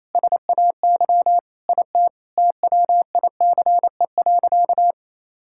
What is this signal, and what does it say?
Morse code, heard as a keyed tone, about 28 wpm, 700 Hz. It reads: SAY IT TWICE.
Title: Sentences from Top 1600 Words 28wpm